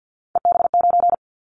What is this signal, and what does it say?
Morse code, heard as a keyed tone, about 50 wpm, 700 Hz. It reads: EB9
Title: Callsign Prefixes for Top 100 DXCC - 50wpm